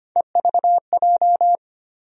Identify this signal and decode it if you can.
Morse code, heard as a keyed tone, about 25 wpm, 700 Hz. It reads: EVJ
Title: USA Call Sign Postfixes - 3 Letters 25wpm